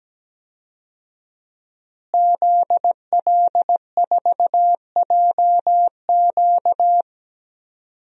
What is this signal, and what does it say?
Morse code, heard as a keyed tone, about 17 wpm, 700 Hz. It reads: ZL4JQ
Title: DXCC-100 Call Signs - 5 Characters - Repeated 3x - 17wpm